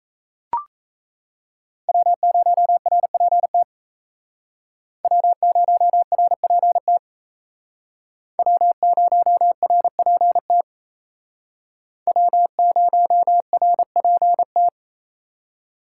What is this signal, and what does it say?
Morse code, heard as a keyed tone, about 35 wpm, 700 Hz. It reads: W0RPT W0RPT W0RPT W0RPT
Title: Speed Racing - USA 1x3 Call Signs 28wpm